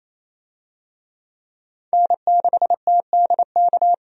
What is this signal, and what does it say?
Morse code, heard as a keyed tone, about 28 wpm, 700 Hz. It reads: N6TDK
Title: USA Call Signs - 5 Characters - 28wpm